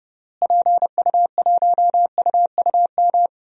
Morse code, heard as a keyed tone, about 32 wpm, 700 Hz. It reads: PU1UUM
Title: DXCC-100 Call Signs - 6 Characters - Repeated 2x - 30wpm